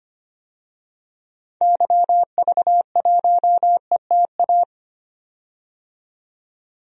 Morse code, heard as a keyed tone, about 25 wpm, 700 Hz. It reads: YV1ETA
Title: DXCC-100 Call Signs - 6 Characters - 25wpm